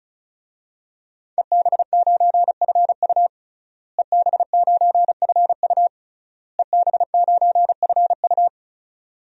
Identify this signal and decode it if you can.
Morse code, heard as a keyed tone, about 36 wpm, 700 Hz. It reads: EB9FU EB9FU EB9FU
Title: DXCC-100 Call Signs - 5 Characters - Repeated 3x - 35wpm